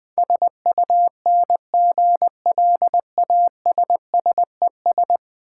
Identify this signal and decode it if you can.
Morse code, heard as a keyed tone, about 20 wpm, 700 Hz. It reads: SUNGLASSES
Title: Easy Words - Speed-Racing with Hints - 20wpm